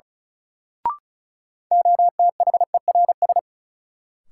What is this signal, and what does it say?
Morse code, heard as a keyed tone, about 36 wpm, 700 Hz. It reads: OTHERS
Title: Sentences from Top 1500 Words - New Word Drill Alt - 35wpm